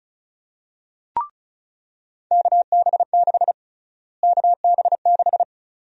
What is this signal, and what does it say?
Morse code, heard as a keyed tone, about 34 wpm, 700 Hz. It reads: KB6 KB6
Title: USA Call Sign Prefixes - Repeated 2x - 35wpm